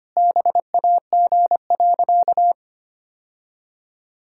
Morse code, heard as a keyed tone, about 25 wpm, 700 Hz. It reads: BAG.
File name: practice / 25wpm / Sentences with 3-Word Phrases - 25wpm